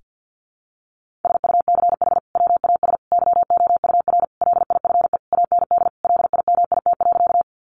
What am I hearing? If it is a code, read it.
Morse code, heard as a keyed tone, about 50 wpm, 700 Hz. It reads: SUCH WAS YOUR LIFE AND LIMIT.
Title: Sentences from Top 1500 Words 50wpm